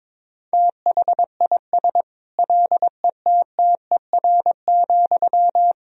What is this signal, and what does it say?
Morse code, heard as a keyed tone, about 22 wpm, 700 Hz. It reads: THIS LETTER,
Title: Sentences from Top 1000 Words - Review Part 4 - 22wpm